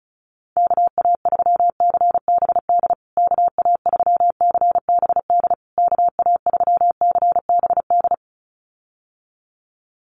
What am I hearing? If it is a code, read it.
Morse code, heard as a keyed tone, about 34 wpm, 700 Hz. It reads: KA3CBD KA3CBD KA3CBD
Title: USA Call Signs Repeated 3x 35wpm